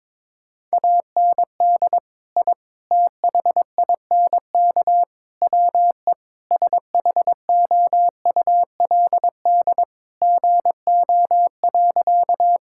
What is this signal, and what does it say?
Morse code, heard as a keyed tone, about 22 wpm, 700 Hz. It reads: AND I THINK WE SHOULD GO.
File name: practice / 22wpm / Sentences with 4-Word Phrases - 22wpm